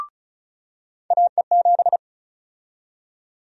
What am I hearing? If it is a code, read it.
Morse code, heard as a keyed tone, about 34 wpm, 700 Hz. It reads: AE7
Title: USA Call Sign Prefixes - 35wpm